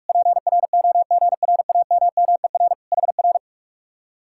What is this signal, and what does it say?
Morse code, heard as a keyed tone, about 57 wpm, 700 Hz. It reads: PROGRAMMER HR